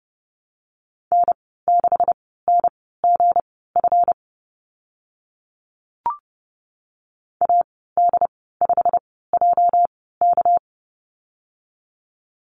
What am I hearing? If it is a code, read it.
Morse code, heard as a keyed tone, about 30 wpm, spaced as for 17 wpm, 700 Hz. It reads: N6NGF AD5JK
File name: Lesson 37g - Letter x - Call Signs 30wpm